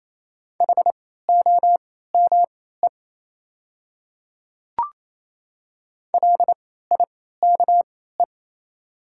Morse code, heard as a keyed tone, about 27 wpm, spaced as for 16 wpm, 700 Hz. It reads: HOME LIKE
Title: 3x Character Spacing - Top 200 Words - 28wpm